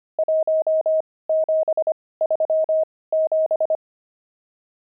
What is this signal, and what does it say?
Morse code, heard as a keyed tone, about 25 wpm, spaced as for 18 wpm, 630 Hz. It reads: 1737
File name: pangram-fs-18-cs-25-nums9